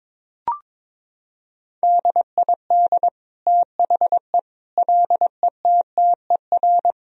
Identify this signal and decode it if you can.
Morse code, heard as a keyed tone, about 22 wpm, 700 Hz. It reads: DID THE LETTER
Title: Sentences from Top 1000 Words - Review Part 2 - 22wpm